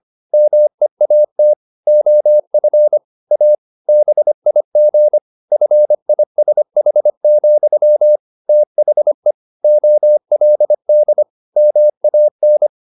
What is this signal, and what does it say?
Morse code, heard as a keyed tone, about 25 wpm, 600 Hz. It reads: MEAT OF A BIG FISH, THE OLD MAN
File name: the old man and the sea part 1 25wpm